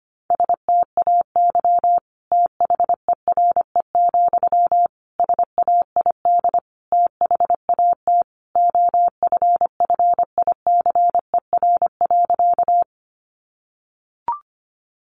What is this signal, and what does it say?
Morse code, heard as a keyed tone, about 25 wpm, 700 Hz. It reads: STAY THERE, SAID THAT OFFICER.
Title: Sentences from Top 1200 Words 25wpm